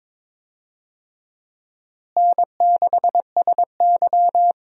Morse code, heard as a keyed tone, about 22 wpm, 700 Hz. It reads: N6SY
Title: USA Call Signs - 4 Characters - Repeated 3x - 22wpm